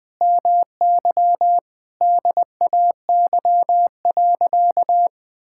Morse code, heard as a keyed tone, about 20 wpm, 700 Hz. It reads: MY DAY.
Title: Sentences with 9-Word Phrases - 20wpm